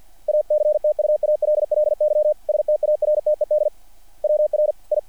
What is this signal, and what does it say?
Morse code, heard as a keyed tone, about 48 wpm, 660 Hz. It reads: ACTUALLY STARTED WRI
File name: archived / CRSnet